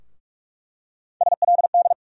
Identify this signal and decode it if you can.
Morse code, heard as a keyed tone, about 46 wpm, 700 Hz. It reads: SLD
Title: USA Call Sign Postfixes - 3 Letters 45wpm